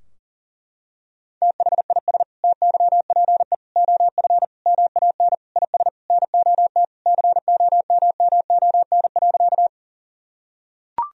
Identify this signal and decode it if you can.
Morse code, heard as a keyed tone, about 40 wpm, 700 Hz. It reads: THIS TYPE OF MAN IS NOT COMMON.
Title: Sentences from Top 1000 Words - Review Part 3 - 40wpm